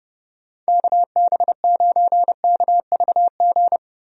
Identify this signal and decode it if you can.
Morse code, heard as a keyed tone, about 30 wpm, 700 Hz. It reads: KB9KVG
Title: USA Call Signs - 6 Characters - Repeated 3x - 30wpm